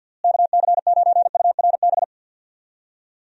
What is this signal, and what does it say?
Morse code, heard as a keyed tone, about 50 wpm, 700 Hz. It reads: KC9URB
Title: USA Call Signs - 6 Characters - Repeated 2x - 50wpm